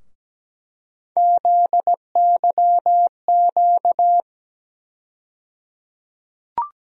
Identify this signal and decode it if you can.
Morse code, heard as a keyed tone, about 17 wpm, 700 Hz. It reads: ZYQ
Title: USA Call Sign Postfixes - 3 Letters 17wpm